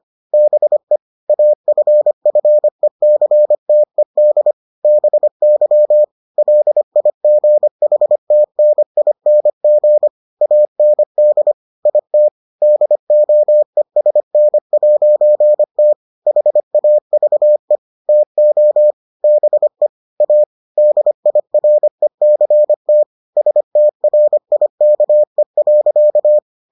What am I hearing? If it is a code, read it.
Morse code, heard as a keyed tone, about 25 wpm, 600 Hz. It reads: BE AFFECTED BY LIGHTNING AND IT DOESN'T HAVE TO BE A DIRECT STRIKE.